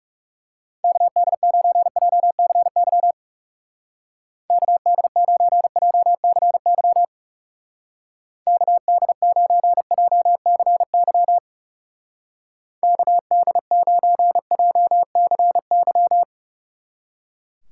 Morse code, heard as a keyed tone, about 39 wpm, 700 Hz. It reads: KD9JCY KD9JCY KD9JCY KD9JCY